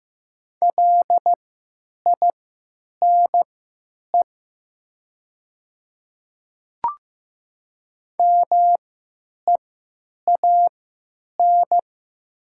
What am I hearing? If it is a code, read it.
Morse code, heard as a keyed tone, about 15 wpm, spaced as for 9 wpm, 700 Hz. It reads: LINE MEAN
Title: 3x Character Spacing - Top 200 Words - 15wpm